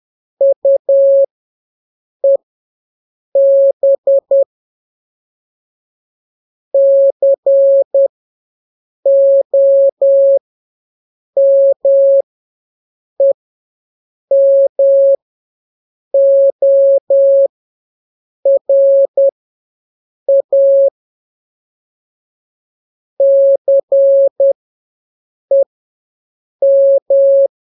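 Morse code, heard as a keyed tone, about 10 wpm, spaced as for 6 wpm, 550 Hz. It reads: UEB COMEMORA CEM
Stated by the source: Telegraphy radio challenge for scouts in the Jamboree-On-The-Air.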